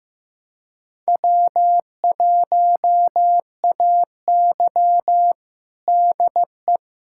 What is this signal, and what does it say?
Morse code, heard as a keyed tone, about 15 wpm, 700 Hz. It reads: W1AY DE
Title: USA Callsign DE Callsign - 4 Characters - 15wpm